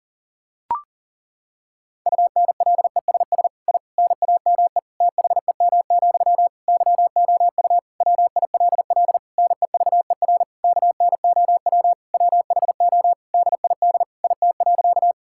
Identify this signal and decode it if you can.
Morse code, heard as a keyed tone, about 40 wpm, 700 Hz. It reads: UNLESS I NAME THEM, YOU WILL NEVER KNOW WHO DID IT.
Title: Sentences from Top 1000 Words - Review Part 6 - 40wpm